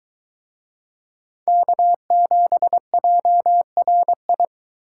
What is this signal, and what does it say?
Morse code, heard as a keyed tone, about 22 wpm, 700 Hz. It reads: K7JRI